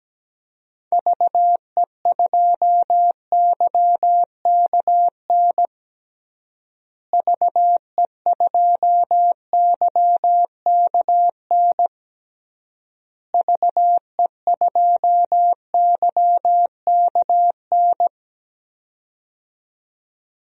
Morse code, heard as a keyed tone, about 17 wpm, 700 Hz. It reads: VE2YKN VE2YKN VE2YKN
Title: DXCC-100 Call Signs - 6 Characters - Repeated 3x - 17wpm